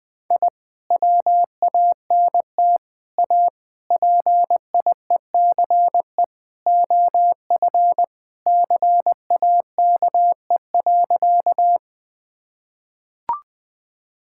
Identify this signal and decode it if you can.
Morse code, heard as a keyed tone, about 20 wpm, 700 Hz. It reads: I WANT A PIECE OF CAKE.
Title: Sentences with 3-Word Phrases - 20wpm